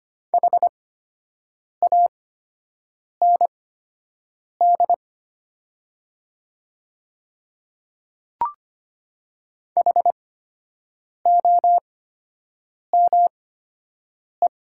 Morse code, heard as a keyed tone, about 25 wpm, spaced as for 7 wpm, 700 Hz. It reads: HAND HOME
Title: 8x Character Spacing - Top 200 Words - 25wpm